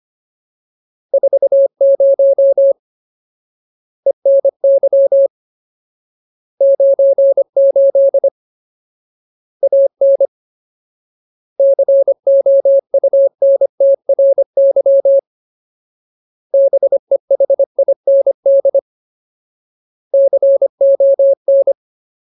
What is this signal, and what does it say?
Morse code, heard as a keyed tone, about 25 wpm, 550 Hz. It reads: 40 ENY 98 AN COUNTRY BEHIND CON
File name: Ultimate Word List / 25WPM Ultimate